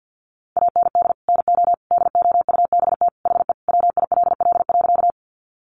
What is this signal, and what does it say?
Morse code, heard as a keyed tone, about 50 wpm, 700 Hz. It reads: AND NO DOUBT HE WILL.
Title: Sentences from Top 1500 Words 50wpm